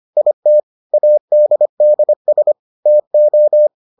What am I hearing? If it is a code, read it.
Morse code, heard as a keyed tone, about 25 wpm, 600 Hz. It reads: IT ADDS TO